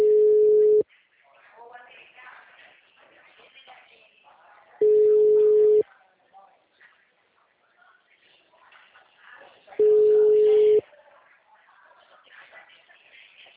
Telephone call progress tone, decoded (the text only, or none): ringback tone